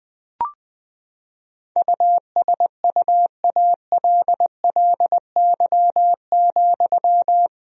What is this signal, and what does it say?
Morse code, heard as a keyed tone, about 21 wpm, 700 Hz. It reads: USUALLY,
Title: Sentences from Top 1000 Words - Review Part 3 - 20wpm